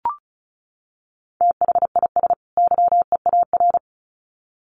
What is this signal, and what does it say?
Morse code, heard as a keyed tone, about 34 wpm, 700 Hz. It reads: THIS YEAR
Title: Sets of 2 Words 35wpm